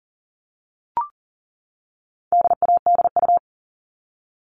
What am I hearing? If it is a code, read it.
Morse code, heard as a keyed tone, about 40 wpm, 700 Hz. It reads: DADU
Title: All Cut Numbers - 4-Digit 40wpm